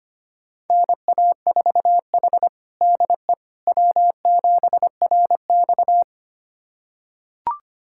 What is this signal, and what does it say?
Morse code, heard as a keyed tone, about 25 wpm, 700 Hz. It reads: NA4H DE W7RX
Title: USA Callsign DE Callsign - 4 Characters - 25wpm